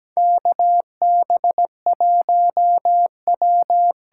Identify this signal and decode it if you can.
Morse code, heard as a keyed tone, about 17 wpm, 700 Hz. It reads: KB1W
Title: USA Call Signs - 6 Characters - 8x Character Spacing - 17wpm